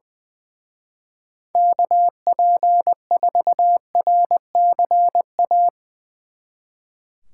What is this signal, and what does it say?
Morse code, heard as a keyed tone, about 21 wpm, 700 Hz. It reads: KP4RCA
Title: Speed Racing - USA 2x3 Call Signs 20wpm